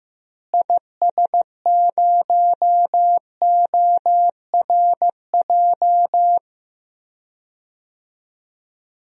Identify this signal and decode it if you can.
Morse code, heard as a keyed tone, about 15 wpm, 700 Hz. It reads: IS0ORJ
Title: DXCC-100 Call Signs - 6 Characters - Repeated 2x - 15wpm